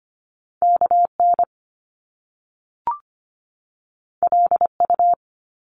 Morse code, heard as a keyed tone, about 25 wpm, 700 Hz. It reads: KN LU
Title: Top Two Letter Combinations in Words 25wpm